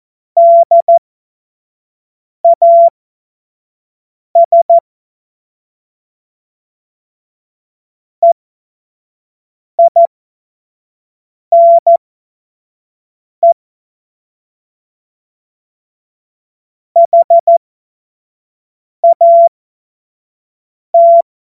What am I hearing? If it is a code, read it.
Morse code, heard as a keyed tone, about 14 wpm, spaced as for 5 wpm, 680 Hz. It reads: DAS EINE HAT
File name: Morsekurs / Witze / Witz 16. Februar 2021, 25 F70